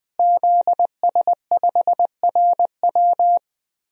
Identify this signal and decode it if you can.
Morse code, heard as a keyed tone, about 20 wpm, 700 Hz. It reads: ZS5RW
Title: DXCC-100 Call Signs - 5 Characters - Repeated 3x - 20wpm